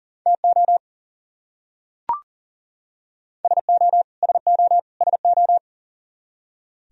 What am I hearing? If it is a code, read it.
Morse code, heard as a keyed tone, about 37 wpm, 700 Hz. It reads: TO SO SO SO
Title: Top 100 Words Repeated 40wpm